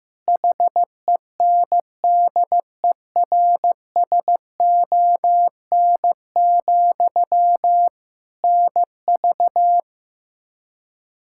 Morse code, heard as a keyed tone, about 15 wpm, 700 Hz. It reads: HENDERSON, NV
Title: Practice QSO - Exchange 1 - Beginning - 15wpm